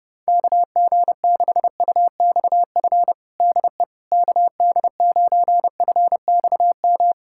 Morse code, heard as a keyed tone, about 30 wpm, 700 Hz. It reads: KG6UXF DE KD9FXM
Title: USA Callsign DE Callsign 30wpm